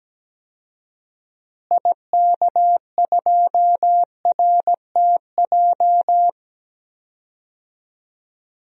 Morse code, heard as a keyed tone, about 17 wpm, 700 Hz. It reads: IK2RTJ
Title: DXCC-100 Call Signs - 6 Characters - Repeated 2x - 17wpm